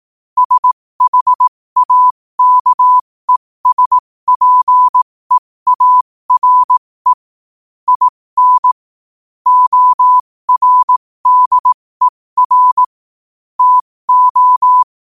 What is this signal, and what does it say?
Morse code, heard as a keyed tone, about 18 wpm, spaced as for 15 wpm, 1000 Hz. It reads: SHAKESPEARE IN ORDER TO
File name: qotdfiles / 15wpm